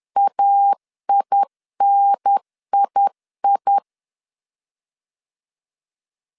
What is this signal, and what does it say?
Morse code, heard as a keyed tone, about 10 wpm, 800 Hz. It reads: AINII
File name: 10wpm